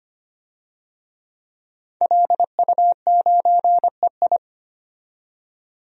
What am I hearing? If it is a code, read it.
Morse code, heard as a keyed tone, about 25 wpm, 700 Hz. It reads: LU9EI
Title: DXCC-100 Call Signs - 5 Characters - 25wpm